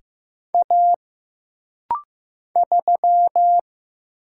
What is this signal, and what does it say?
Morse code, heard as a keyed tone, about 15 wpm, 700 Hz. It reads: A 3
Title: Single Letter-Number Alt - Rapid-Fire 15wpm